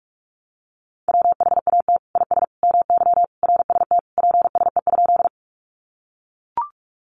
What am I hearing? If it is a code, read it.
Morse code, heard as a keyed tone, about 44 wpm, 700 Hz. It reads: WHAT IS MY RST PSE?
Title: Common QSO Phrases 45wpm